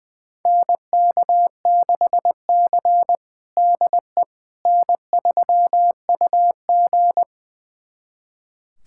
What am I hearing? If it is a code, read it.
Morse code, heard as a keyed tone, about 20 wpm, 700 Hz. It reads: NK6C DE N3UG
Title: USA Callsign DE Callsign - 4 Characters - 20wpm